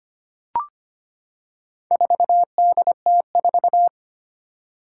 Morse code, heard as a keyed tone, about 25 wpm, 700 Hz. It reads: 4DT4
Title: Four Challenging Letter-Number Combinations 25wpm